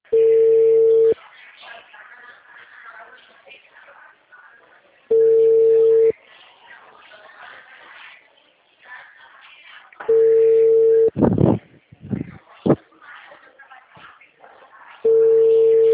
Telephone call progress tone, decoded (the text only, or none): ringback tone